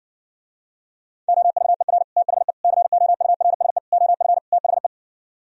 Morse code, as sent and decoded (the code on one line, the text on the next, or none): --- ...- . .-. / - .... / -.-. --- ..- .-. ... / --- ..-. / - .... .
OVER THE COURSE OF THE